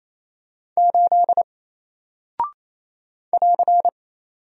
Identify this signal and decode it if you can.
Morse code, heard as a keyed tone, about 27 wpm, 700 Hz. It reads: 8 <AR>
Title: Single Character (Letter:Number:Punctuation:Prosign) Alt 28wpm